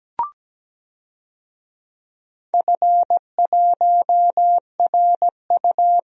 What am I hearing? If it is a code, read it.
Morse code, heard as a keyed tone, about 17 wpm, 700 Hz. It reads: F1RU
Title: DXCC-100 Call Signs - 4 Characters - Repeated 3x - 17wpm